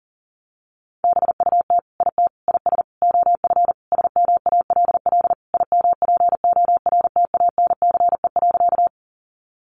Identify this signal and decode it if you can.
Morse code, heard as a keyed tone, about 40 wpm, 700 Hz. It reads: BUT IT IS OF SMALL IMPORTANCE.
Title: Sentences from Top 1200 Words 40wpm